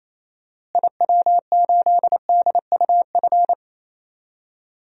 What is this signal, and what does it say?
Morse code, heard as a keyed tone, about 28 wpm, 700 Hz. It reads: IW8DUF
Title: DXCC-100 Call Signs - 6 Characters - Repeated 2x - 28wpm